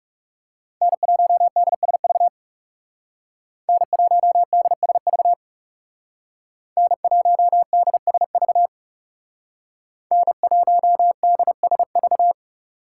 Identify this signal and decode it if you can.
Morse code, heard as a keyed tone, about 37 wpm, 700 Hz. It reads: N1DSV N1DSV N1DSV N1DSV
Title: Speed Racing - USA 1x3 Call Signs 30wpm